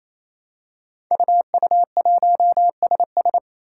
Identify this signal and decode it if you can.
Morse code, heard as a keyed tone, about 27 wpm, 700 Hz. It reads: UU1SS
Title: Five Challenging Letter-Number Combinations 28wpm